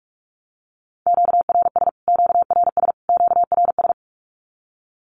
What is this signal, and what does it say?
Morse code, heard as a keyed tone, about 44 wpm, 700 Hz. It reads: QRS QRS QRS
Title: Q-Codes Repeated 3x 45wpm